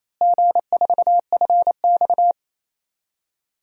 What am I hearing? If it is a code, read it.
Morse code, heard as a keyed tone, about 28 wpm, 700 Hz. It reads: G4FX
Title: DXCC-100 Call Signs - 4 Characters - Repeated 3x - 28wpm